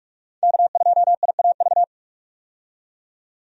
Morse code, heard as a keyed tone, about 47 wpm, 700 Hz. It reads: K2IAV
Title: USA Call Signs - 5 Characters - 45wpm